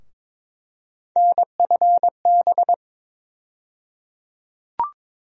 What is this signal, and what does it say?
Morse code, heard as a keyed tone, about 22 wpm, 700 Hz. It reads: NFB
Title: USA Call Sign Postfixes - 3 Letters 22wpm